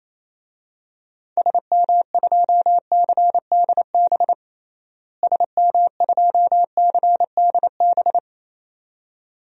Morse code, heard as a keyed tone, about 28 wpm, 700 Hz. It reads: SM2CDB SM2CDB
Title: DXCC-100 Call Signs - 6 Characters - Repeated 2x - 28wpm